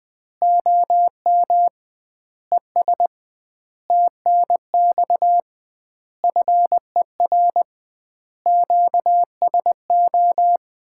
Morse code, heard as a keyed tone, about 20 wpm, 700 Hz. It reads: OM ES TNX FER QSO
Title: Common QSO Phrases - 2x Word-Spacing - 20wpm